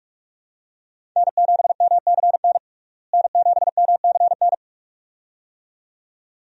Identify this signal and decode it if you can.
Morse code, heard as a keyed tone, about 47 wpm, 700 Hz. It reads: N7MCN N7MCN
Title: USA Call Signs - 5 Characters - Repeated 2x - 45wpm